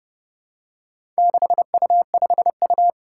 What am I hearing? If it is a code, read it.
Morse code, heard as a keyed tone, about 30 wpm, 700 Hz. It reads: 6U5U
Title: Four Challenging Letter-Number Combinations 30wpm